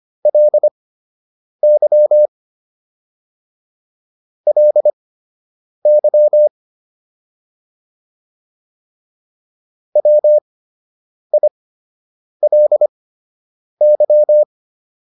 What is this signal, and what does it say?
Morse code, heard as a keyed tone, about 25 wpm, spaced as for 8 wpm, 600 Hz. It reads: LY LY WILY